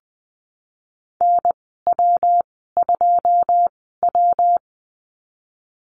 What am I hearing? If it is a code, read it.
Morse code, heard as a keyed tone, about 20 wpm, spaced as for 15 wpm, 700 Hz. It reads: NW2W
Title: 122 Lesson 20g - Number 3 - Call Signs 20wpm